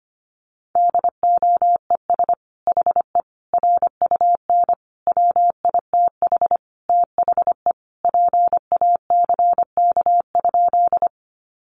Morse code, heard as a keyed tone, about 25 wpm, 700 Hz. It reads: DOES HE RUN WITH THE PACK?
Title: Sentences from Top 1300 Words 25wpm